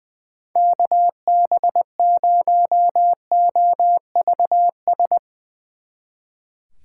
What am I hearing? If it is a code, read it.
Morse code, heard as a keyed tone, about 21 wpm, 700 Hz. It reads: KB0OVS
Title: Speed Racing - USA 2x3 Call Signs 20wpm